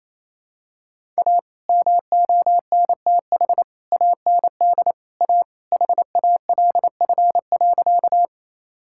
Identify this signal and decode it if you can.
Morse code, heard as a keyed tone, about 28 wpm, 700 Hz. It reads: A MONTH AND A HALF.
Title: Sentences from Top 1000 Words - Review Part 5 - 28wpm